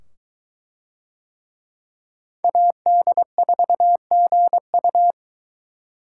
Morse code, heard as a keyed tone, about 21 wpm, 700 Hz. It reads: AD4GU